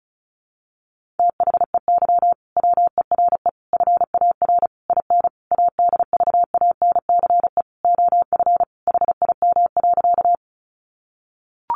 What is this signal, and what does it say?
Morse code, heard as a keyed tone, about 34 wpm, 700 Hz. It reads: THEY WERE FAR IN ADVANCE OF HIM.
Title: Sentences from Top 1400 Words 35wpm